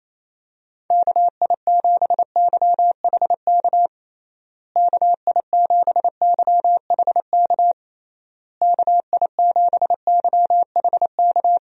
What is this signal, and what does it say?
Morse code, heard as a keyed tone, about 28 wpm, 700 Hz. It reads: KI7YHK KI7YHK KI7YHK
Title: USA Call Signs - 6 Characters - Repeated 3x - 28wpm